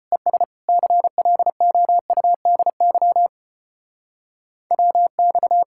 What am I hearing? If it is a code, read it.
Morse code, heard as a keyed tone, about 33 wpm, 700 Hz. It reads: ES CLOUDY WX